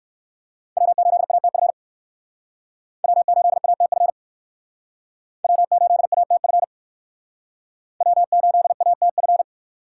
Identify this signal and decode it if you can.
Morse code, heard as a keyed tone, about 58 wpm, 700 Hz. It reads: W8ATF W8ATF W8ATF W8ATF